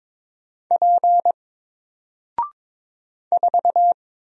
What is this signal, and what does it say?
Morse code, heard as a keyed tone, about 22 wpm, 700 Hz. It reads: P 4
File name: Single-Letter-Number-22wpm